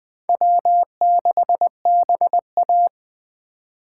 Morse code, heard as a keyed tone, about 20 wpm, 700 Hz. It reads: W6BA
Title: Speed Racing - USA Call Signs 17wpm